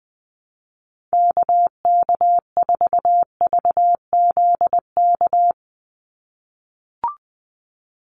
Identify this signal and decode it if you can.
Morse code, heard as a keyed tone, about 20 wpm, 700 Hz. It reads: KK4VZK
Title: USA Call Signs 20wpm